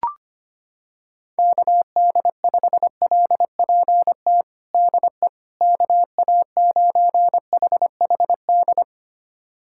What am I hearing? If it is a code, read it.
Morse code, heard as a keyed tone, about 25 wpm, 700 Hz. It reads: KD5LPT DE KA9HHD
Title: USA Callsign DE Callsign 25wpm